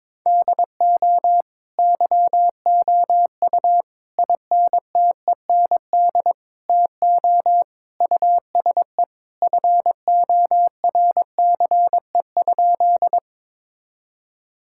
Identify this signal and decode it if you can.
Morse code, heard as a keyed tone, about 22 wpm, 700 Hz. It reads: DO YOU INTEND TO USE FORCE?
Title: Sentences from Top 1600 Words 22wpm